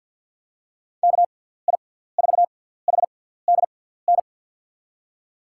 Morse code, heard as a keyed tone, about 47 wpm, spaced as for 17 wpm, 700 Hz. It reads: KI4HDN
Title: USA Call Signs - 6 Characters - 6x Character Spacing - 50wpm